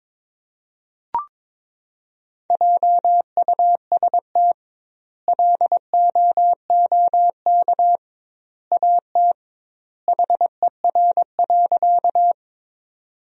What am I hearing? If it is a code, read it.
Morse code, heard as a keyed tone, about 22 wpm, 700 Hz. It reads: JUST LOOK AT HER.
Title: 2x Word-Spacing - Sentences from Top 200 Words - 22wpm